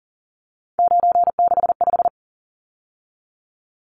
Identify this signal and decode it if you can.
Morse code, heard as a keyed tone, about 40 wpm, 700 Hz. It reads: 965
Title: Numbers - 3-Digit 40wpm